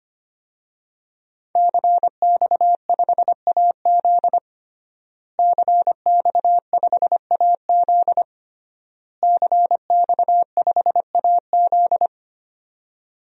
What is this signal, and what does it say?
Morse code, heard as a keyed tone, about 25 wpm, 700 Hz. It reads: CX5AZ CX5AZ CX5AZ
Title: DXCC-100 Call Signs - 5 Characters - Repeated 3x - 25wpm